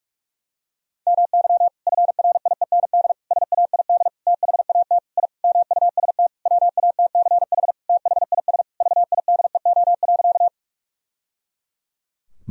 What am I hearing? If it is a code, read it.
Morse code, heard as a keyed tone, about 41 wpm, 700 Hz. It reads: MY FRIEND SAID THAT I MUST WATCH THIS VIDEO.
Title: Sentences from Top 1000 Words - Review Part 4 - 45wpm